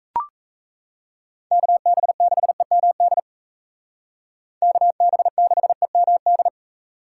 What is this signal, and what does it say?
Morse code, heard as a keyed tone, about 35 wpm, 700 Hz. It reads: KB6EMD KB6EMD
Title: Speed Racing - USA 2x3 Call Signs 28wpm